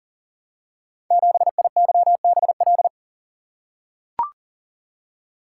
Morse code, heard as a keyed tone, about 39 wpm, 700 Hz. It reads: 7IYBL
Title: Code Groups - 5 Letter:Numbers - 40wpm